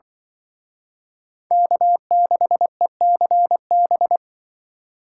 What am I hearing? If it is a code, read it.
Morse code, heard as a keyed tone, about 24 wpm, 700 Hz. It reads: K6ECB
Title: Speed Racing - USA 1x3 Call Signs 20wpm